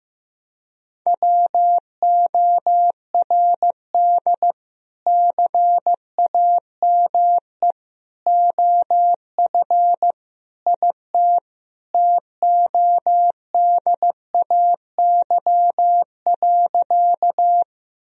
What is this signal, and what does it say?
Morse code, heard as a keyed tone, about 15 wpm, 700 Hz. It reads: WORD CAME OF IT TODAY.
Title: Sentences from Top 1000 Words - Review Part 4 - 15wpm